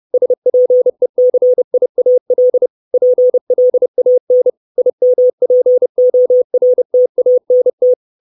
Morse code, heard as a keyed tone, about 30 wpm, 500 Hz. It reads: SPECIAL PLAN IMPORTANT
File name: 632 Words 30WPM